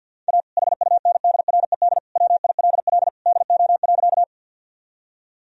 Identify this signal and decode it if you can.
Morse code, heard as a keyed tone, about 52 wpm, 700 Hz. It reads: A HUNDRED WILL DO.
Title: Sentences from Top 1000 Words - Review Part 2 - 50wpm